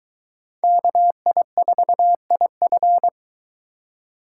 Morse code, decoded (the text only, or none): KI4IF